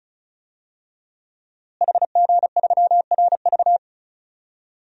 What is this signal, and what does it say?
Morse code, heard as a keyed tone, about 36 wpm, 700 Hz. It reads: HG3RV
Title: DXCC-100 Call Signs - 5 Characters - Repeated 2x - 35wpm